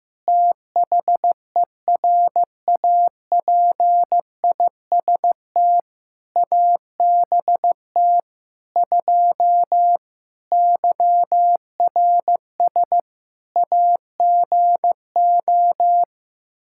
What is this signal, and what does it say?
Morse code, heard as a keyed tone, about 15 wpm, 700 Hz. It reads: THERAPIST ABT 2 YRS AGO
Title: K1USN SST QSO Practice 15wpm